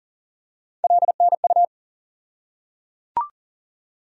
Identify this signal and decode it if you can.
Morse code, heard as a keyed tone, about 38 wpm, 700 Hz. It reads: LNU
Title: USA Call Sign Postfixes - 3 Letters 40wpm